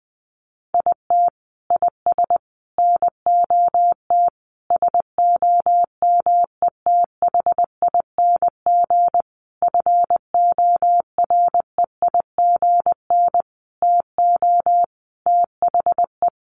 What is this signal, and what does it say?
Morse code, heard as a keyed tone, about 20 wpm, 700 Hz. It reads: IT IS NOT SOMETHING FOREIGN TO THE
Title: Sentences from Top 1400 Words 20wpm